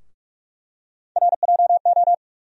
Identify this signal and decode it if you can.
Morse code, heard as a keyed tone, about 45 wpm, 700 Hz. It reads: RJO
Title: USA Call Sign Postfixes - 3 Letters 45wpm